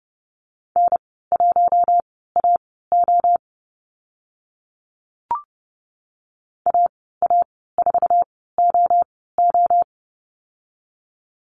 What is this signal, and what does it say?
Morse code, heard as a keyed tone, about 30 wpm, spaced as for 17 wpm, 700 Hz. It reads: N1AO AA4OO
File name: Lesson 15g - Number 5 - Call Signs 30wpm